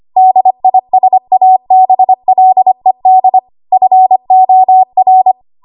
Morse code, heard as a keyed tone, about 25 wpm, 750 Hz. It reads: DISABLED FOR
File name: Archive / 25 WPM / 25WPM